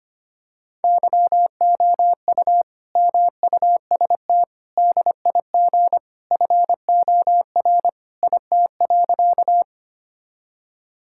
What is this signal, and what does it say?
Morse code, heard as a keyed tone, about 25 wpm, 700 Hz. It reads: YOU MUST DIG FOR IT.
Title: Sentences from Top 1600 Words 25wpm